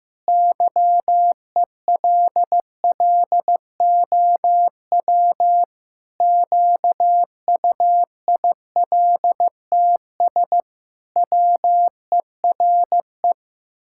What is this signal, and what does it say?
Morse code, heard as a keyed tone, about 15 wpm, 700 Hz. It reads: YELLOW QUILTS WERE
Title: Pangram Challenge v2 - 15wpm